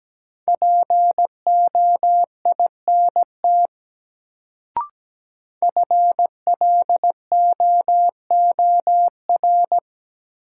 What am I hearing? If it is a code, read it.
Morse code, heard as a keyed tone, about 17 wpm, 700 Hz. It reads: POINT FLOOR
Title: Sets of 3 Words - New Word Drill - 17wpm